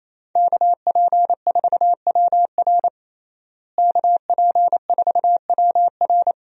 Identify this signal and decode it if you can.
Morse code, heard as a keyed tone, about 27 wpm, 700 Hz. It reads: KP4WR KP4WR
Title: USA Call Signs - 5 Characters - Repeated 2x - 28wpm